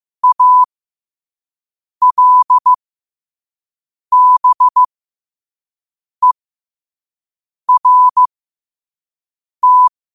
Morse code, heard as a keyed tone, about 15 wpm, spaced as for 5 wpm, 1000 Hz. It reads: ALBERT
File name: qotdfiles / 5wpm